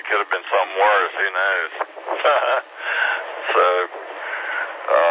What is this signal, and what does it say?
Emergency Alert System (SAME) tones, heard as harsh buzzing bursts, with speech. It is heard from 1.1 to 4.7 s.